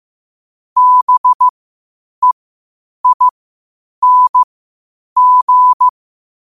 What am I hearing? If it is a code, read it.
Morse code, heard as a keyed tone, about 15 wpm, spaced as for 8 wpm, 1000 Hz. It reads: BEING
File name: qotdfiles / 8wpm